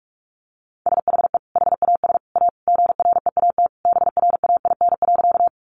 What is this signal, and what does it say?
Morse code, heard as a keyed tone, about 44 wpm, 700 Hz. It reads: SHE HAS A GREAT BRAIN.
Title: Sentences from Top 1100 Words 45wpm